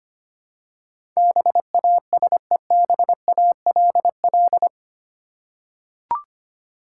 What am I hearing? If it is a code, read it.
Morse code, heard as a keyed tone, about 25 wpm, 700 Hz. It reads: BASEBALL
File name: Eight Letter Words - 25wpm